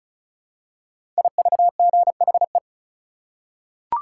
Morse code, heard as a keyed tone, about 36 wpm, 700 Hz. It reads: IVGHE
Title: Code Groups - 5 Letters - 35wpm